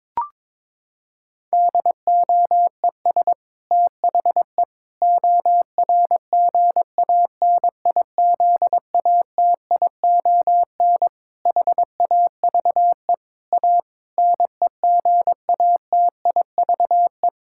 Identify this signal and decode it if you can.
Morse code, heard as a keyed tone, about 22 wpm, 700 Hz. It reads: DOES THE ORGANIZATION HAVE A NEGATIVE
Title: Sentences from Top 1000 Words - Review Part 6 - 22wpm